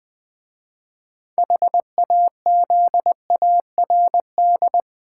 Morse code, heard as a keyed tone, about 20 wpm, 700 Hz. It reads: HAZARD
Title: 6 Letter Words - First 3 Letters Given - 20wpm